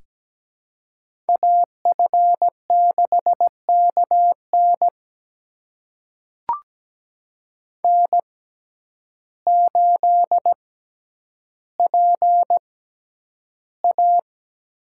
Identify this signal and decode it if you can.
Morse code, heard as a keyed tone, about 17 wpm, spaced as for 9 wpm, 700 Hz. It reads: AF6KN N8PA
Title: USA Call Signs - 5 Characters - 6x Character Spacing - 17wpm